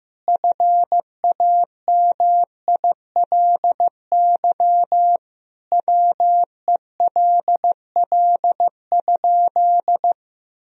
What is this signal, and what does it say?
Morse code, heard as a keyed tone, about 16 wpm, 700 Hz. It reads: FAMILY WELL?
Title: Sentences from Top 400 Words and Abbreviations 15wpm